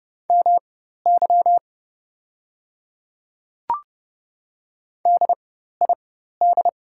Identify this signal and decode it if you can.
Morse code, heard as a keyed tone, about 30 wpm, spaced as for 14 wpm, 690 Hz. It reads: MY DID
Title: 4x Character Spacing - Top 100 Words - 30wpm